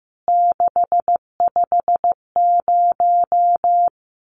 Morse code, heard as a keyed tone, about 15 wpm, 700 Hz. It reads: 650
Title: CWT Exchange 15wpm